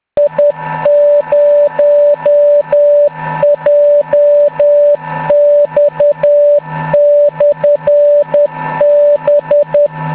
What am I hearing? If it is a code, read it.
Morse code, heard as a keyed tone, about 10 wpm, 580 Hz. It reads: I0JX/B